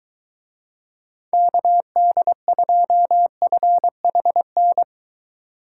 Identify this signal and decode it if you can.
Morse code, heard as a keyed tone, about 23 wpm, 700 Hz. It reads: KD2FHN